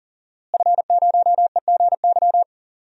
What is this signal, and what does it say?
Morse code, heard as a keyed tone, about 38 wpm, 700 Hz. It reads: F0EGY
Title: DXCC-100 Call Signs - 5 Characters - Repeated 2x - 40wpm